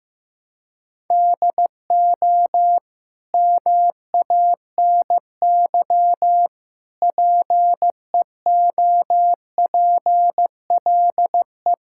Morse code, heard as a keyed tone, about 15 wpm, 700 Hz. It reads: DO MANY PEOPLE
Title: Sentences from Top 1000 Words - Review Part 6 - 15wpm